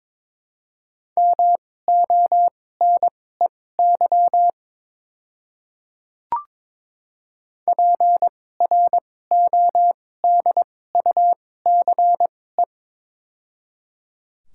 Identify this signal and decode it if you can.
Morse code, heard as a keyed tone, about 21 wpm, spaced as for 16 wpm, 700 Hz. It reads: MONEY PRODUCE
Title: 2x Character Spacing - Top 500 Words - 22wpm